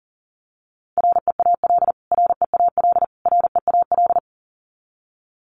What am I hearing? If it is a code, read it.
Morse code, heard as a keyed tone, about 40 wpm, 700 Hz. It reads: REAL REAL REAL
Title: Top 500 Words Repeated 40wpm